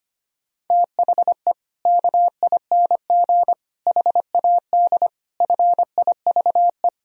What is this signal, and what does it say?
Morse code, heard as a keyed tone, about 25 wpm, 700 Hz. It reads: THE KING HAD FIVE
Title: Sentences from Top 1000 Words - Review Part 2 - 25wpm